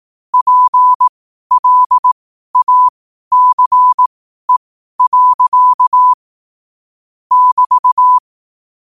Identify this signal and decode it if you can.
Morse code, heard as a keyed tone, about 18 wpm, spaced as for 12 wpm, 1000 Hz. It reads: PLACE. <BT>
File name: qotdfiles / 12wpm